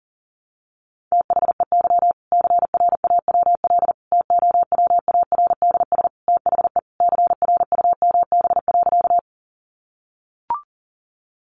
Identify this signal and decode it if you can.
Morse code, heard as a keyed tone, about 40 wpm, 700 Hz. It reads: THEY CRAWL TOWARDS THE CRUMB.
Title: Additional Easy Phrases 40wpm